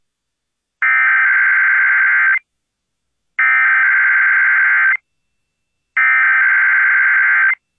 Emergency Alert System (SAME) tones (header bursts), heard as harsh buzzing bursts. It is heard from 0.8 to 7.5 s.